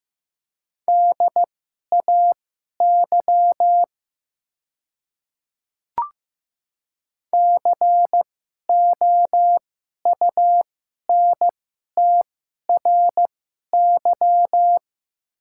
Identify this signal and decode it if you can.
Morse code, heard as a keyed tone, about 15 wpm, spaced as for 11 wpm, 700 Hz. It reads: DAY COUNTRY
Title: 2x Character Spacing - Top 300 Words - 15wpm